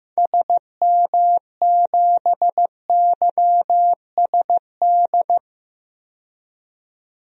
Morse code, heard as a keyed tone, about 15 wpm, 690 Hz. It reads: SM7YSD
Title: DXCC-100 Call Signs - 6 Characters - 15wpm